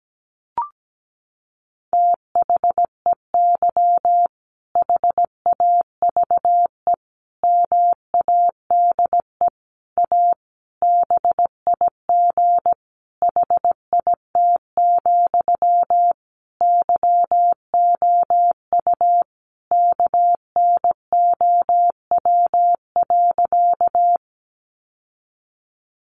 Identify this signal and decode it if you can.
Morse code, heard as a keyed tone, about 17 wpm, 700 Hz. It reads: THEY HAVE MADE A BIG HIT, YOU KNOW.
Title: Sentences from Top 1200 Words 17wpm